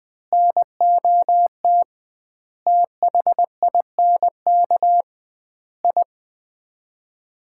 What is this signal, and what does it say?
Morse code, heard as a keyed tone, about 20 wpm, 700 Hz. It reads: NOT THINK I
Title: 2x Word-Spacing - Sets of 5 Words - 20wpm